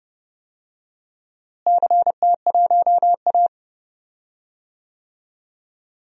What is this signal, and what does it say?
Morse code, heard as a keyed tone, about 30 wpm, 700 Hz. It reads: CT1A
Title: DXCC-100 Call Signs - 4 Characters - Repeated 3x - 30wpm